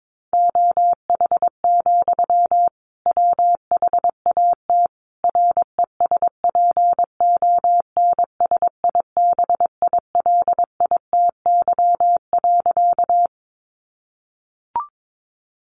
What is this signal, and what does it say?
Morse code, heard as a keyed tone, about 22 wpm, 700 Hz. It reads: OH, WHAT RESPONSIBILITY.
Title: Sentences from Top 1200 Words 22wpm